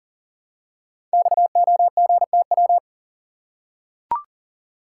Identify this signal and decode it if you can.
Morse code, heard as a keyed tone, about 38 wpm, 690 Hz. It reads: XOGTW
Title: Code Groups - 5 Letter:Numbers - 40wpm